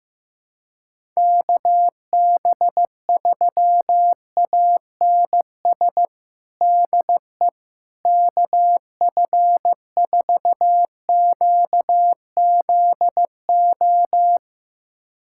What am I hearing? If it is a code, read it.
Morse code, heard as a keyed tone, about 16 wpm, 700 Hz. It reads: KB3ANS DE KF4QZO
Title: USA Callsign DE Callsign - 6 Characters - 15wpm